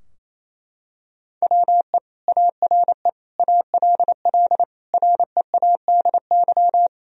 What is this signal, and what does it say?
Morse code, heard as a keyed tone, about 27 wpm, 700 Hz. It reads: WE ARE ALL READY
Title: Sentences from Top 1000 Words - Review Part 4 - 28wpm